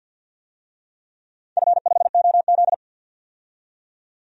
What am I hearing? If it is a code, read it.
Morse code, heard as a keyed tone, about 51 wpm, 700 Hz. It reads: F5OZ
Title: DXCC-100 Call Signs - 4 Characters - 50wpm